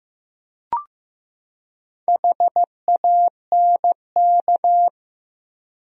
Morse code, heard as a keyed tone, about 15 wpm, 700 Hz. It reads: HANK
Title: USA Common Names - 15wpm